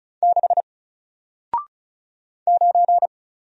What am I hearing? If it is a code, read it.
Morse code, heard as a keyed tone, about 36 wpm, 700 Hz. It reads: 6 9
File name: Single-Letter-Number-35wpm